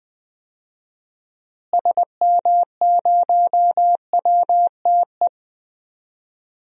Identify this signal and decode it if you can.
Morse code, heard as a keyed tone, about 21 wpm, 700 Hz. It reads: SM0WTE
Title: DXCC-100 Call Signs - 6 Characters - Repeated 2x - 20wpm